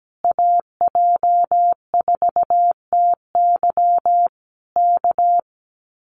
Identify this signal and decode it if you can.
Morse code, heard as a keyed tone, about 17 wpm, 700 Hz. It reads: AJ4TY K
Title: Common QSO Phrases Encore 17wpm